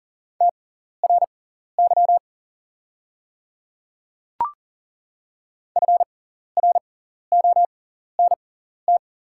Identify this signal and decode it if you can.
Morse code, heard as a keyed tone, about 37 wpm, spaced as for 14 wpm, 700 Hz. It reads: TRY FRONT
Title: 6x Character Spacing - Top 500 Words - 40wpm